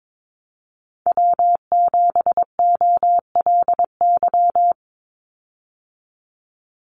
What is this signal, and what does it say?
Morse code, heard as a keyed tone, about 22 wpm, 700 Hz. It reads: W7OLY
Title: Call Signs Repeated 3x - 22wpm